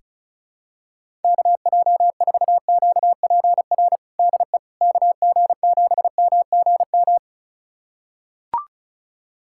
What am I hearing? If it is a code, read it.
Morse code, heard as a keyed tone, about 36 wpm, 700 Hz. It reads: KJ4QPR DE KG7MGM
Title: USA Callsign DE Callsign 35wpm